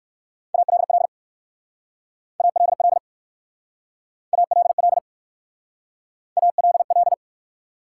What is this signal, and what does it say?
Morse code, heard as a keyed tone, about 60 wpm, 700 Hz. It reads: ALL ALL ALL ALL